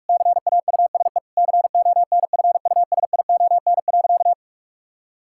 Morse code, heard as a keyed tone, about 41 wpm, 700 Hz. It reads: CAUSE CONFUSION.
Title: Sentences from Top 1600 Words 45wpm